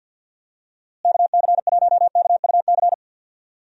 Morse code, heard as a keyed tone, about 50 wpm, 700 Hz. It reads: KC1KUC
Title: USA Call Signs - 6 Characters - Repeated 3x - 50wpm